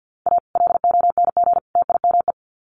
Morse code, heard as a keyed tone, about 50 wpm, 700 Hz. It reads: A LONG TIME
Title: Sets of 3 Words 50wpm